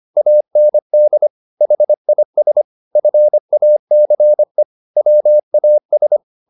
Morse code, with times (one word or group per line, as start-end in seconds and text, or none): 0.2-1.3 AND
1.6-2.6 HIS
3.0-4.6 FACE
5.0-6.2 WAS